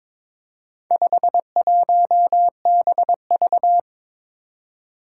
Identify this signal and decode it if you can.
Morse code, heard as a keyed tone, about 22 wpm, 700 Hz. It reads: 51BV
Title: Four Challenging Letter-Number Combinations 22wpm